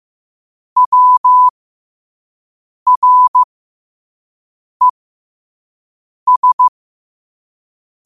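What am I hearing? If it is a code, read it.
Morse code, heard as a keyed tone, about 15 wpm, spaced as for 5 wpm, 1000 Hz. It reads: WRES
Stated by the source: Morse code quote of the day sent at 5 wpm.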